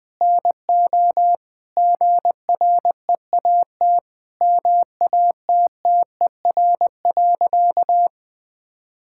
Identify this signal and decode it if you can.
Morse code, heard as a keyed tone, about 21 wpm, 700 Hz. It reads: NO GREAT MATTER.
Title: Sentences from Top 1600 Words 20wpm